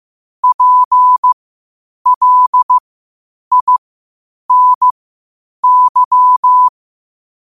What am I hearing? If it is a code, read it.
Morse code, heard as a keyed tone, about 15 wpm, spaced as for 8 wpm, 1000 Hz. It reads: PLINY
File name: qotdfiles / 8wpm